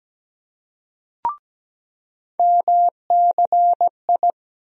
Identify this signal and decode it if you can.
Morse code, heard as a keyed tone, about 17 wpm, 700 Hz. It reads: MCI
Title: USA Call Sign Postfixes - 3 Letters 17wpm